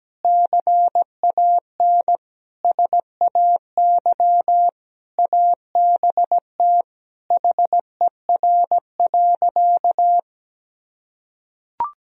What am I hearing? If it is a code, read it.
Morse code, heard as a keyed tone, about 17 wpm, 700 Hz. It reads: CAN SAY ABT HER.
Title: Sentences from Top 200 Words and Abbreviations 17wpm